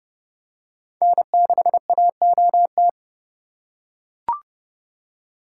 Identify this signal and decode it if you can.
Morse code, heard as a keyed tone, about 31 wpm, 700 Hz. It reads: N6AOT
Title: USA Call Signs - 2x Character Spacing - 30wpm